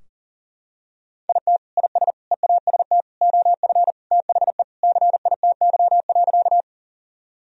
Morse code, heard as a keyed tone, about 40 wpm, 700 Hz. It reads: IT IS EAST OF THE CITY.
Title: Sentences from Top 1000 Words - Review Part 3 - 40wpm